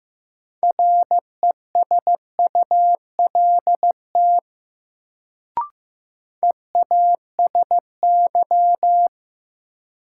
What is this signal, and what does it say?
Morse code, heard as a keyed tone, about 16 wpm, 700 Hz. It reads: RESULT EASY
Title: Sentences from Top 700 Words - New Word Drill - 15wpm